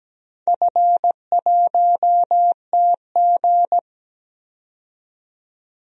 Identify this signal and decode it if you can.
Morse code, heard as a keyed tone, about 17 wpm, 700 Hz. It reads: F1TG
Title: DXCC-100 Call Signs - 4 Characters - 17wpm